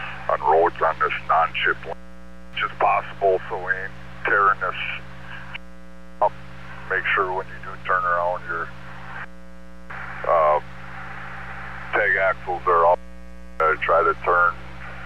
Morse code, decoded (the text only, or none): none